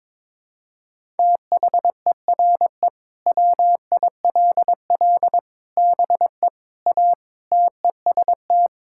Morse code, heard as a keyed tone, about 22 wpm, 700 Hz. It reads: THERE WILL BE A TEST
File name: practice / 22wpm / Sentences with 3-Word Phrases - 22wpm